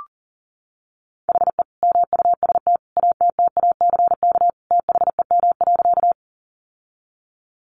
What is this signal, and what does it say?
Morse code, heard as a keyed tone, about 40 wpm, 700 Hz. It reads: HE MUST ATTACK THEM.
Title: Sentences from Top 1500 Words 40wpm